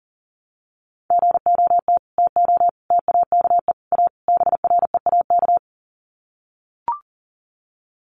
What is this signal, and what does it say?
Morse code, heard as a keyed tone, about 40 wpm, 700 Hz. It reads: GOT TO TAKE A BREAK
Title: Sets of 5 Words Encore 40wpm